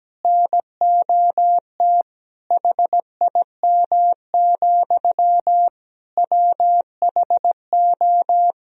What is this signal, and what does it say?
Morse code, heard as a keyed tone, about 17 wpm, 700 Hz. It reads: NOT HIM, WHO
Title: Sentences from Top 1000 Words - Review Part 2 - 17wpm